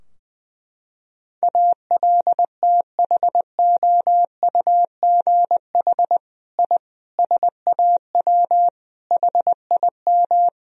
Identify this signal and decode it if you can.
Morse code, heard as a keyed tone, about 19 wpm, 700 Hz. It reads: ALTHOUGH I SAW HIM
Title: Sentences from Top 1000 Words - Review Part 3 - 20wpm